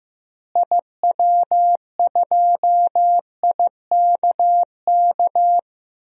Morse code, heard as a keyed tone, about 15 wpm, 700 Hz. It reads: IW2IKK
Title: DXCC-100 Call Signs - 6 Characters - Repeated 2x - 15wpm